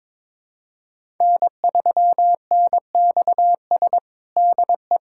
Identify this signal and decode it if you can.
Morse code, heard as a keyed tone, about 22 wpm, 700 Hz. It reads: N3NXS DE
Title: USA Callsign DE Callsign - 5 Characters - 22wpm